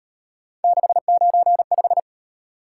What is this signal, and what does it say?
Morse code, heard as a keyed tone, about 38 wpm, 700 Hz. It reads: 695